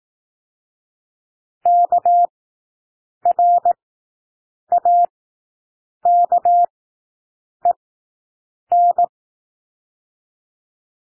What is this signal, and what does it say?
Morse code, heard as a keyed tone, about 18 wpm, spaced as for 7 wpm, 700 Hz. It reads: KRAKEN
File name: SOUND WORDS 18 wpm